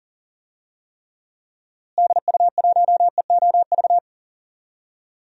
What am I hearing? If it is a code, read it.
Morse code, heard as a keyed tone, about 37 wpm, 700 Hz. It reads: DU1EOV
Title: DXCC-100 Call Signs - 6 Characters - Repeated 2x - 40wpm